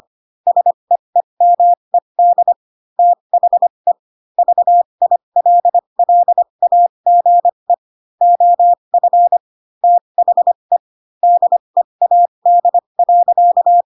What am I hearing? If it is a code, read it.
Morse code, heard as a keyed tone, about 25 wpm, spaced as for 22 wpm, 700 Hz. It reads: SEEMED THE VILLAGE OF THE DEAD.
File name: Pinocchio / Pinocchio ch6 18 25 ik1hge